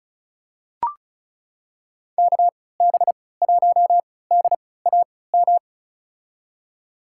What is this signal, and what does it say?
Morse code, heard as a keyed tone, about 36 wpm, spaced as for 20 wpm, 700 Hz. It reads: KB1DAM
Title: USA Call Signs - 6 Characters - 3x Character Spacing - 35wpm